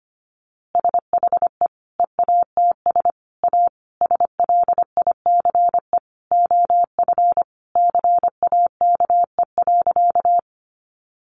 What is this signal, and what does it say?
Morse code, heard as a keyed tone, about 25 wpm, 700 Hz. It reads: SHE EATS A SLICE OF CAKE.
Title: Easy Phrases 25wpm